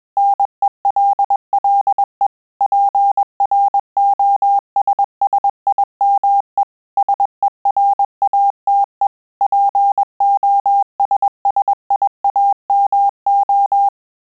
Morse code, heard as a keyed tone, about 21 wpm, 800 Hz. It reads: NELLE PROSSIME SERATE POSSIAMO
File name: Telegraphy-group / breve testo midi